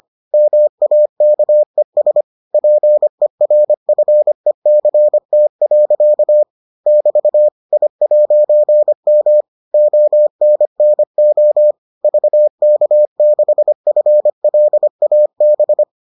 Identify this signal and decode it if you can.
Morse code, heard as a keyed tone, about 25 wpm, 600 Hz. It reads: MAKES PERFECT. <BT> I'M ONNO VK6FLAB